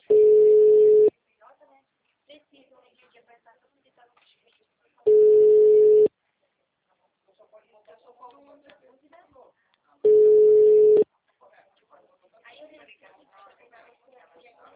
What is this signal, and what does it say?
Ringback tone (Europe, 425 Hz), with speech.
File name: out- -2009- - - .1984